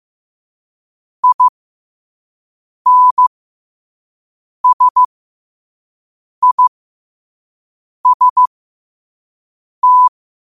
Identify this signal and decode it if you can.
Morse code, heard as a keyed tone, about 15 wpm, spaced as for 5 wpm, 1000 Hz. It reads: INSIST
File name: qotdfiles / 5wpm